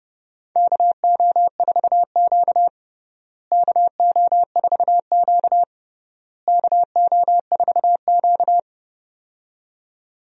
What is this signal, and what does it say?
Morse code, heard as a keyed tone, about 29 wpm, 700 Hz. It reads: KO4Q KO4Q KO4Q
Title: USA Call Signs - 4 Characters - Repeated 3x - 30wpm